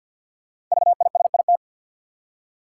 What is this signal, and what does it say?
Morse code, heard as a keyed tone, about 48 wpm, 700 Hz. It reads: VISIT
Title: Five Letter Words - 50wpm